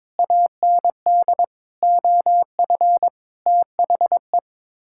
Morse code, heard as a keyed tone, about 22 wpm, 700 Hz. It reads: AND OF THE
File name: Sets of 9 Words - 22wpm